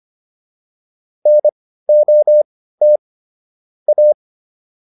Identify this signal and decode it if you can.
Morse code, heard as a keyed tone, about 25 wpm, spaced as for 15 wpm, 600 Hz. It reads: NOT A